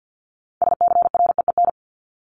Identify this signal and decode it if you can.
Morse code, heard as a keyed tone, about 48 wpm, 700 Hz. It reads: SCREEN
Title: Top 751-1000 Words 50wpm